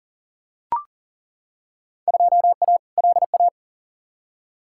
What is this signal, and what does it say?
Morse code, heard as a keyed tone, about 40 wpm, 700 Hz. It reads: 2A LA
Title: Field Day Exchanges - 40wpm